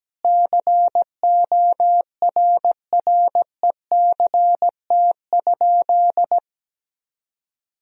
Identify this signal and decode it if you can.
Morse code, heard as a keyed tone, about 17 wpm, 700 Hz. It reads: CORRECT?
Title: Sentences from Top 1000 Words - Review Part 2 - 17wpm